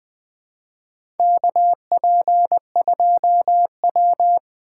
Morse code, heard as a keyed tone, about 20 wpm, 700 Hz. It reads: KP2W
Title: USA Call Signs - 6x Character Spacing - 20wpm